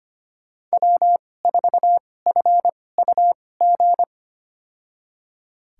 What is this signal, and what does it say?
Morse code, heard as a keyed tone, about 25 wpm, spaced as for 18 wpm, 700 Hz. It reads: W4FUG
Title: USA Call Signs - 5 Characters - 2x Character Spacing - 25wpm